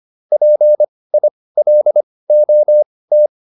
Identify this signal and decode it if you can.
Morse code, heard as a keyed tone, about 25 wpm, spaced as for 18 wpm, 600 Hz. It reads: PILOT